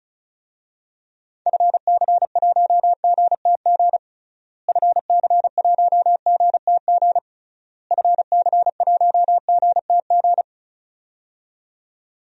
Morse code, heard as a keyed tone, about 36 wpm, 700 Hz. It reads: FC1GTG FC1GTG FC1GTG
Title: DXCC-100 Call Signs - 6 Characters - Repeated 3x - 35wpm